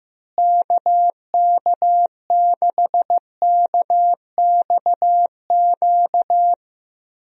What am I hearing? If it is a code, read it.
Morse code, heard as a keyed tone, about 15 wpm, 700 Hz. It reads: KK6KXQ
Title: USA Call Signs - 6 Characters - Repeated 3x - 15wpm